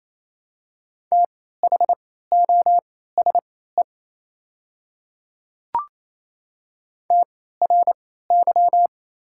Morse code, heard as a keyed tone, about 28 wpm, spaced as for 16 wpm, 700 Hz. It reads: THOSE TRY
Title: 3x Character Spacing - Top 300 Words - 28wpm